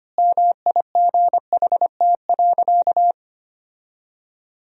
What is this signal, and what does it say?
Morse code, heard as a keyed tone, about 25 wpm, 700 Hz. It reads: MIGHT.
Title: 4x Word-Spacing - Sentences from Top 500 Words - 25wpm